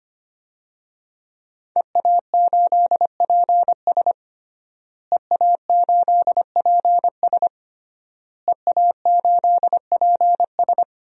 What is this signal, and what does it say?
Morse code, heard as a keyed tone, about 25 wpm, 700 Hz. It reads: EA8PS EA8PS EA8PS
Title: DXCC-100 Call Signs - 5 Characters - Repeated 3x - 25wpm